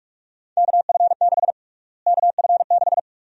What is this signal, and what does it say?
Morse code, heard as a keyed tone, about 43 wpm, 700 Hz. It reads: KF6 KF6
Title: USA Call Sign Prefixes - Repeated 2x - 45wpm